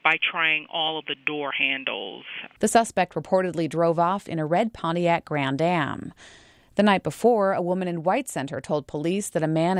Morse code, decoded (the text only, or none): none